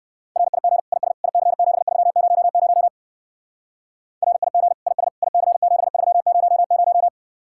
Morse code, heard as a keyed tone, about 56 wpm, 700 Hz. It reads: RIG IS IC7300 RIG IS IC7300